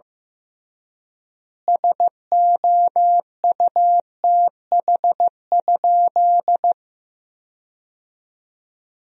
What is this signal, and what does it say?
Morse code, heard as a keyed tone, about 15 wpm, 700 Hz. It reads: SOUTH?
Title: 3x Word-Spacing - Sentences from Top 400 Words - 15wpm